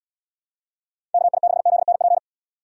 Morse code, heard as a keyed tone, about 78 wpm, 700 Hz. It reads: KI6ZNP